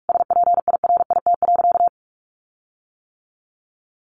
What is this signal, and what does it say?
Morse code, heard as a keyed tone, about 44 wpm, 700 Hz. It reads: SPIRIT.
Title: Sentences from Top 1400 Words 45wpm